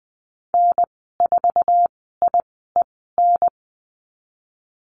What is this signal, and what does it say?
Morse code, heard as a keyed tone, about 19 wpm, spaced as for 14 wpm, 700 Hz. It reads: N4IEN
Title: 140 Lesson 23g - Letter f - Call Signs 20wpm